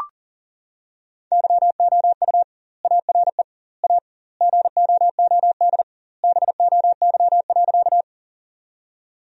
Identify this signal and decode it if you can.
Morse code, heard as a keyed tone, about 37 wpm, 700 Hz. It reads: YOU ARE A GOOD BOY.
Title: 2x Word-Spacing - Sentences from Top 200 Words - 40wpm